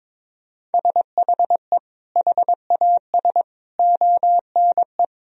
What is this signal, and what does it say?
Morse code, heard as a keyed tone, about 22 wpm, 700 Hz. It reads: SHE HAS ONE
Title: Sentences from Top 1000 Words - Review Part 4 - 22wpm